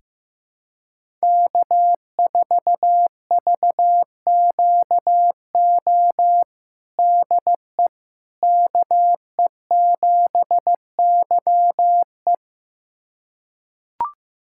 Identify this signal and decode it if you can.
Morse code, heard as a keyed tone, about 16 wpm, 700 Hz. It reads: K4VQO DE KE7YE
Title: USA Callsign DE Callsign - 5 Characters - 15wpm